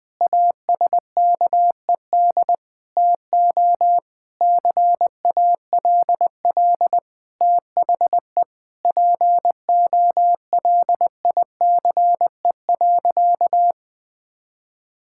Morse code, heard as a keyed tone, about 21 wpm, 700 Hz. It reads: ASKED TO CALL THE POLICE.
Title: Sentences with 7-Word Phrases - Spoken Beginning - 20wpm